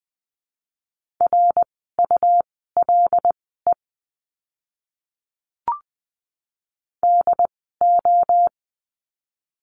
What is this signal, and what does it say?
Morse code, heard as a keyed tone, about 20 wpm, spaced as for 15 wpm, 700 Hz. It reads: RULE DO
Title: 101 Lesson 17f - Letter u - Words 20wpm